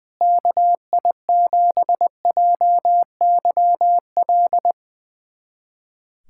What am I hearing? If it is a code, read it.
Morse code, heard as a keyed tone, about 20 wpm, 700 Hz. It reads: KI7JYL
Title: USA Callsign DE Callsign 20wpm